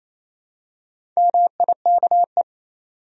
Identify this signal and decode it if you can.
Morse code, heard as a keyed tone, about 28 wpm, 700 Hz. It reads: MIKE
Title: USA Common Names - Max 5 Letters - 28wpm